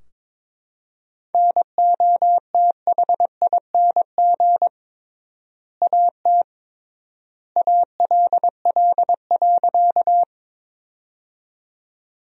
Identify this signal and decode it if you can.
Morse code, heard as a keyed tone, about 22 wpm, 700 Hz. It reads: NOTHING AT ALL.
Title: 3x Word-Spacing - Sentences from Top 500 Words - 22wpm